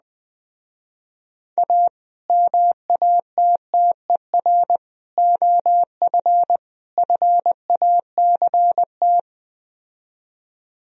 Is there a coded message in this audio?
Yes: Morse code from 1.6 to 9.2 s, heard as a keyed tone, about 21 wpm, 700 Hz. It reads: A MATTER OF FACT